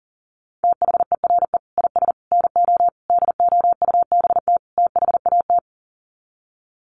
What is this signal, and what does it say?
Morse code, heard as a keyed tone, about 40 wpm, 700 Hz. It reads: THERE IS NO DOUBT THAT
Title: Sets of 5 Words 40wpm